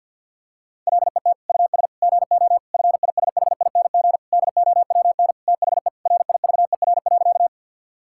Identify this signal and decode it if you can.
Morse code, heard as a keyed tone, about 50 wpm, 700 Hz. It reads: LET US GO FISHING DOWN THE RIVER.
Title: Sentences from Top 1000 Words - Review Part 5 - 50wpm